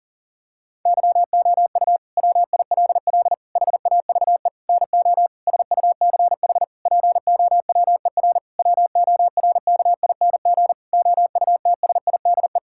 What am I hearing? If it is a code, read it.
Morse code, heard as a keyed tone, about 40 wpm, 700 Hz. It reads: YOU WILL HAVE NO SUCH POWER WORKING OUTSIDE
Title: Sentences from Top 1000 Words - Review Part 6 - 40wpm